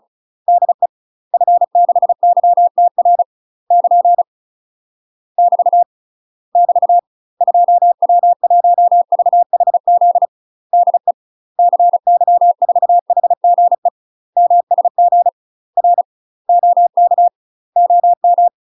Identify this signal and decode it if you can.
Morse code, heard as a keyed tone, about 34 wpm, 700 Hz. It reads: DE F6YTR <KN> <BT> <BT> 2W1VHZ DE CY4HGE MSG R OK OM
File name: UK MASTER QSOs TRAINING / 35WPM / UK MASTER QSOs 0024